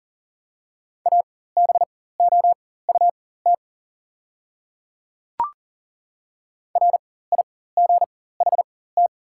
Morse code, heard as a keyed tone, about 39 wpm, spaced as for 18 wpm, 700 Hz. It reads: ABOUT RIGHT
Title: 4x Character Spacing - Top 300 Words - 40wpm